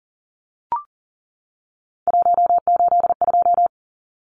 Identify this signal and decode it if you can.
Morse code, heard as a keyed tone, about 40 wpm, 700 Hz. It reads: 182
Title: Numbers - 3-Digit 40wpm